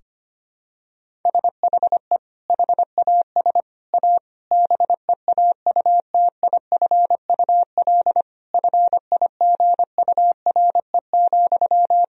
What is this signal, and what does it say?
Morse code, heard as a keyed tone, about 25 wpm, 700 Hz. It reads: SHE HAS A BEAUTIFUL FIGURE,
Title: Sentences from Top 1000 Words - Review Part 2 - 25wpm